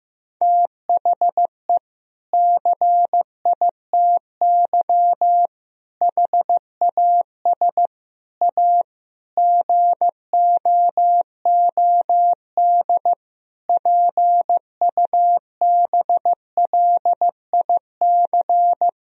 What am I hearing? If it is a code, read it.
Morse code, heard as a keyed tone, about 15 wpm, 700 Hz. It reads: THE CITY HAS A GOOD PUBLIC
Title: Sentences from Top 1000 Words - Review Part 4 - 15wpm